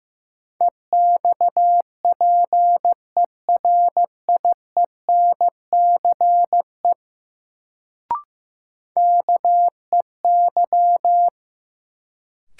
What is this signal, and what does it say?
Morse code, heard as a keyed tone, about 15 wpm, 700 Hz. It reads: EXPERIENCE KEY
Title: Sentences from Top 600 Words - New Word Drill - 15wpm